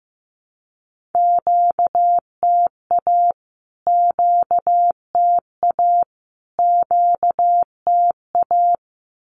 Morse code, heard as a keyed tone, about 15 wpm, 700 Hz. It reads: QTA QTA QTA
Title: Q-Codes Repeated 3x 15wpm